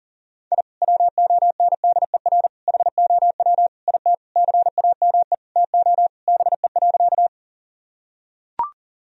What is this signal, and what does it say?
Morse code, heard as a keyed tone, about 40 wpm, 700 Hz. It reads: I WONDER HOW IT CAME TO BE.
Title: Sentences from Top 1000 Words - Review Part 3 - 40wpm